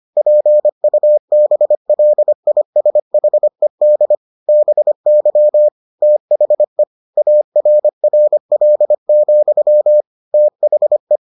Morse code, heard as a keyed tone, about 25 wpm, 600 Hz. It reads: PUBLISHED BY THE ARRL, THE